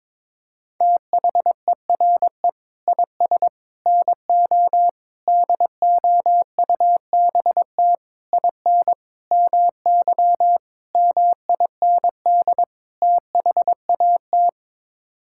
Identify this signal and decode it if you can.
Morse code, heard as a keyed tone, about 22 wpm, 700 Hz. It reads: THERE IS NO DOUBT IN MY MIND THAT
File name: practice / 22wpm / Sets of 8 Words - 22wpm